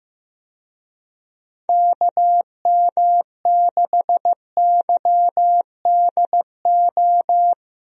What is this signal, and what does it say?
Morse code, heard as a keyed tone, about 15 wpm, 700 Hz. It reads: KM6YDO
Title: USA Call Signs - 6 Characters - 15wpm